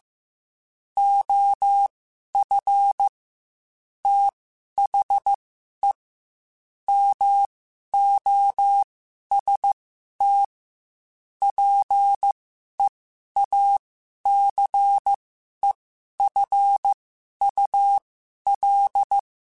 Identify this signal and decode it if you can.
Morse code, heard as a keyed tone, about 15 wpm, spaced as for 11 wpm, 800 Hz. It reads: OF THE MOST PEACEFUL
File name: Quotes / Quotes-mp3s / ChildAsleep @13